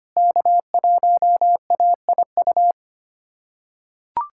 Morse code, heard as a keyed tone, about 25 wpm, 700 Hz. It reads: K1AIU
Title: USA Callsign DE Callsign - 5 Characters - 25wpm